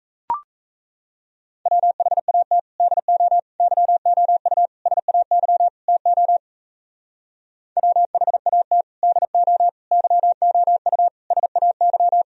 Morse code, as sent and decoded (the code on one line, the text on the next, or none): .-- .... .- - / -.. --- / -.-- --- ..- / ... .- -.-- / - --- / .-- .... .- - / -.. --- / -.-- --- ..- / ... .- -.--
WHAT DO YOU SAY TO WHAT DO YOU SAY